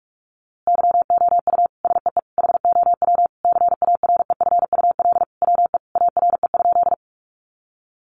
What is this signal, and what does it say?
Morse code, heard as a keyed tone, about 44 wpm, 700 Hz. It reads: YOU SEE HOW CAREFUL WE ARE?
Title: A Sound Of Thunder 45wpm